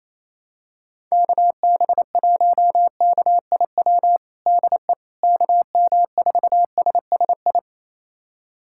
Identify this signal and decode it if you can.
Morse code, heard as a keyed tone, about 28 wpm, 700 Hz. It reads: KB1KIW DE KM4SSI
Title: USA Callsign DE Callsign 28wpm